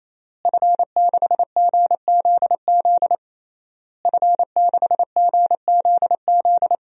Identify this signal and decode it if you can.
Morse code, heard as a keyed tone, about 28 wpm, 700 Hz. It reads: F6GZZ F6GZZ
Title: DXCC-100 Call Signs - 5 Characters - Repeated 3x - 28wpm